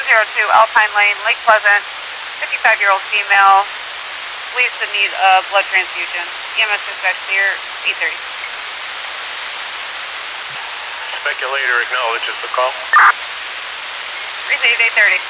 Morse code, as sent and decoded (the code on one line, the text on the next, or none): none